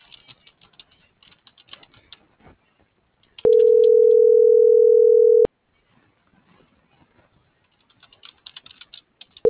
Ringback tone (North America), 440 + 480 Hz.